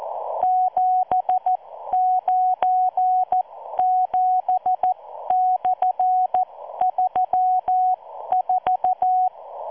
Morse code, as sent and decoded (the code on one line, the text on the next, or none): --... ----. --... -..-. ...-- ....-
797/34